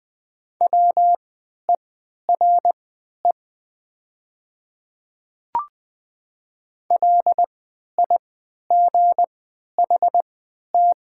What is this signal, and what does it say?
Morse code, heard as a keyed tone, about 20 wpm, spaced as for 11 wpm, 690 Hz. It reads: WERE LIGHT
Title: 3x Character Spacing - Top 200 Words - 20wpm